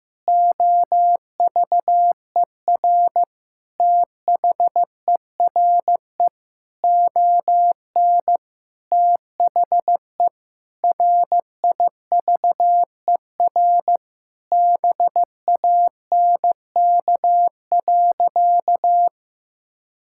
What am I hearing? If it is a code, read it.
Morse code, heard as a keyed tone, about 15 wpm, 700 Hz. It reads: OVER THERE ON THE RIVER BANK.
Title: Sentences from Top 1000 Words - Review Part 6 - 15wpm